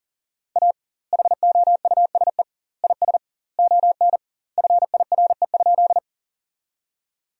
Morse code, heard as a keyed tone, about 40 wpm, 700 Hz. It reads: A HOUSE IS ON FIRE?
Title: 2x Word-Spacing - Sentences from Top 400 Words - 40wpm